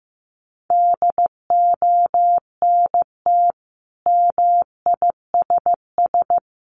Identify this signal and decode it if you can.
Morse code, heard as a keyed tone, about 15 wpm, 700 Hz. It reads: DONT MISS
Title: Sentences from Top 800 Words 15wpm